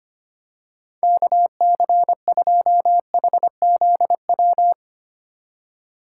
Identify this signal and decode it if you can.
Morse code, heard as a keyed tone, about 25 wpm, 700 Hz. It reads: KC2HZW
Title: USA Call Signs - 6 Characters - 3x Character Spacing - 25wpm